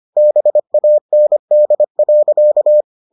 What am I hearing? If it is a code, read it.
Morse code, heard as a keyed tone, about 25 wpm, 600 Hz. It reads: BAND.